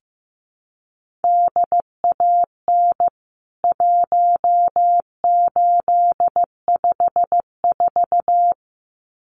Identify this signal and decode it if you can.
Morse code, heard as a keyed tone, about 15 wpm, 700 Hz. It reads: DAN 1854
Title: CWT Exchange 15wpm